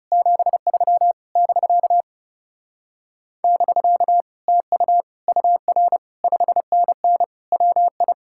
Morse code, heard as a keyed tone, about 40 wpm, 700 Hz. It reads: 73 <BK> <BK> TU UR 5NN WI